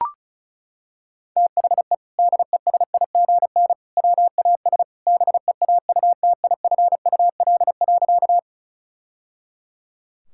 Morse code, heard as a keyed tone, about 34 wpm, 700 Hz. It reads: THE DESIGN WAS BEAUTIFUL.
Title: Sentences from Top 1000 Words - Review Part 5 - 35wpm